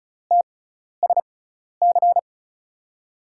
Morse code, heard as a keyed tone, about 36 wpm, spaced as for 12 wpm, 700 Hz. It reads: TSC
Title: USA Call Signs - 6 Characters - 6x Character Spacing - 35wpm